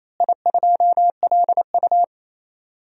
Morse code, heard as a keyed tone, about 28 wpm, 700 Hz. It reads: I2LU
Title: DXCC-100 Call Signs - 4 Characters - Repeated 2x - 28wpm